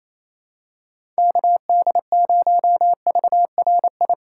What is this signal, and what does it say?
Morse code, heard as a keyed tone, about 28 wpm, 700 Hz. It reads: KD0VRI
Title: USA Call Signs - 6 Characters - 6x Character Spacing - 28wpm